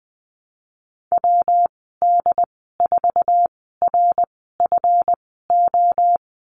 Morse code, heard as a keyed tone, about 20 wpm, spaced as for 15 wpm, 700 Hz. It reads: WD4RFO
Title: 154 Lesson 25g - Letter p - Call Signs 20wpm